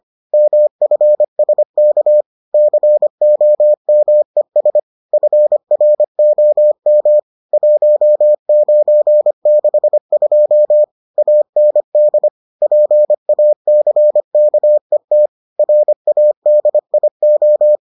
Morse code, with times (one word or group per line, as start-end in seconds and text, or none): 0.3-2.2 MFSK
2.5-4.8 COMES
5.1-7.2 FROM
7.5-10.8 1962
11.2-12.3 AND
12.6-15.3 PACKET
15.6-17.8 RADIO